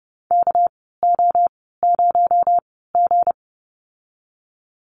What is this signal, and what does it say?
Morse code, heard as a keyed tone, about 30 wpm, spaced as for 17 wpm, 700 Hz. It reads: KO0G
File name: Lesson 37g - Letter x - Call Signs 30wpm